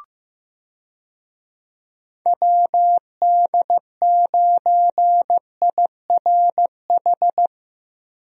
Morse code, heard as a keyed tone, about 15 wpm, 700 Hz. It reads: WD9IRH
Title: USA Call Signs - 6 Characters - 15wpm